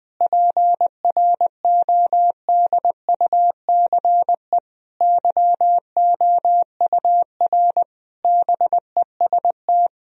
Morse code, heard as a keyed tone, about 21 wpm, 700 Hz. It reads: PRODUCE YOUR BEST
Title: Sentences from Top 1000 Words - Review Part 3 - 20wpm